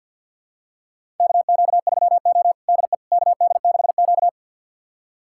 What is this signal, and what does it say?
Morse code, heard as a keyed tone, about 52 wpm, 700 Hz. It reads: KQ3O DE KD6Q
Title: USA Callsign DE Callsign - 4 Characters - 50wpm